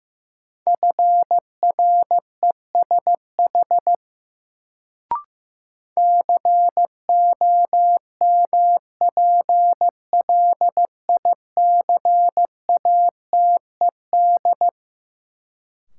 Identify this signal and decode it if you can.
Morse code, heard as a keyed tone, about 15 wpm, 700 Hz. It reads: FRESH COMPLICATED
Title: Sentences from Top 1200 Words - New Word Drill - 15wpm